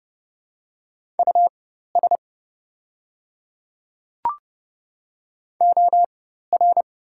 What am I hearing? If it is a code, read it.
Morse code, heard as a keyed tone, about 30 wpm, spaced as for 14 wpm, 700 Hz. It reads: US OR
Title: 4x Character Spacing - Top 200 Words - 30wpm